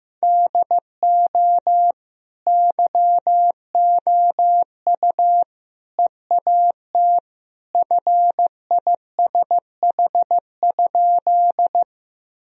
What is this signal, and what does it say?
Morse code, heard as a keyed tone, about 15 wpm, 700 Hz. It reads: DO YOU EAT FISH?
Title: Sentences from Top 1000 Words - Review Part 1 - 15wpm